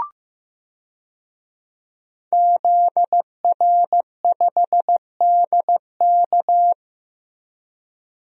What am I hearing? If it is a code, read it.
Morse code, heard as a keyed tone, about 15 wpm, 700 Hz. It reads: ZR5DK
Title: DXCC-100 Call Signs - 5 Characters - Repeated 2x - 15wpm